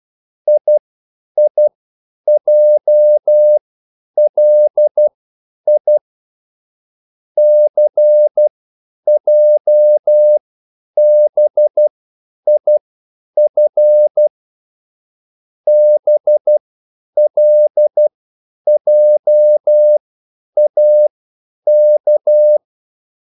Morse code, heard as a keyed tone, about 12 wpm, spaced as for 9 wpm, 600 Hz. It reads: IIJLI CJBIF BLJAK
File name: QSL / CW lesson